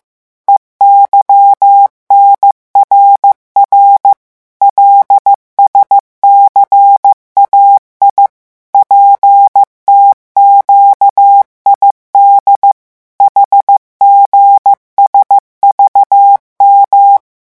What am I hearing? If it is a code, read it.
Morse code, heard as a keyed tone, about 15 wpm, 800 Hz. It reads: EYNRR LSCAI PTQID HGSVM
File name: morse mp3 / revision7-vitesse 14 mots